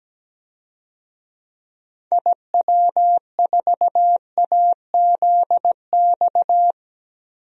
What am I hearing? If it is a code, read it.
Morse code, heard as a keyed tone, about 17 wpm, 700 Hz. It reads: IW4AZX
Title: DXCC-100 Call Signs - 6 Characters - 17wpm